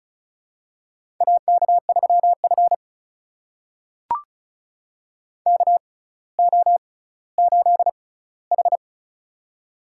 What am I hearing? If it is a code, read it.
Morse code, heard as a keyed tone, about 36 wpm, spaced as for 18 wpm, 700 Hz. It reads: AK3F KO8H
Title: USA Call Signs - 6x Character Spacing - 35wpm